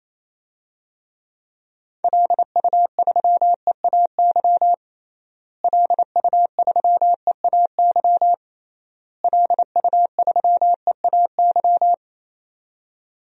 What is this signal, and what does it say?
Morse code, heard as a keyed tone, about 28 wpm, 700 Hz. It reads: LU3EAY LU3EAY LU3EAY
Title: DXCC-100 Call Signs - 6 Characters - Repeated 3x - 28wpm